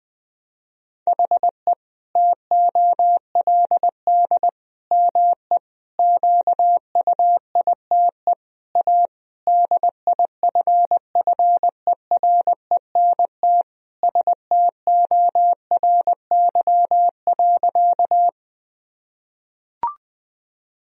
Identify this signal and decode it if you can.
Morse code, heard as a keyed tone, about 20 wpm, 700 Hz. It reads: HE TOLD ME QUITE A DIFFERENT STORY.
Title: Sentences from Top 1000 Words - Review Part 3 - 20wpm